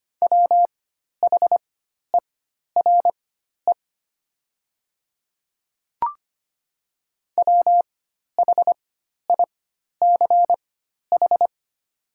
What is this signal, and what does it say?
Morse code, heard as a keyed tone, about 25 wpm, spaced as for 12 wpm, 700 Hz. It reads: WHERE WHICH
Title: 4x Character Spacing - Top 200 Words - 25wpm